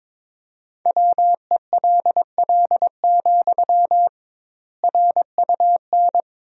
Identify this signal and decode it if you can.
Morse code, heard as a keyed tone, about 22 wpm, 700 Hz. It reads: WELL, RUN
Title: 2x Word-Spacing - Sentences from Top 300 Words - 22wpm